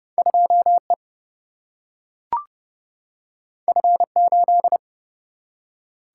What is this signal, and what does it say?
Morse code, heard as a keyed tone, about 30 wpm, 700 Hz. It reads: 2 E F 8
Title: Single-Letter Single-Number 30wpm